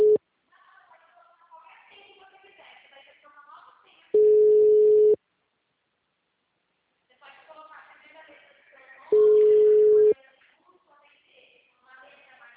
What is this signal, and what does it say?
Ringback tone (Europe, 425 Hz), with speech.
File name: out- -2011- - - .58535